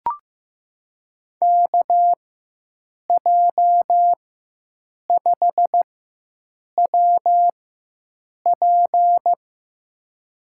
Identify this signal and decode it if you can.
Morse code, heard as a keyed tone, about 15 wpm, spaced as for 7 wpm, 700 Hz. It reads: KJ5WP
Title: USA Call Signs - 5 Characters - 4x Character Spacing - 15wpm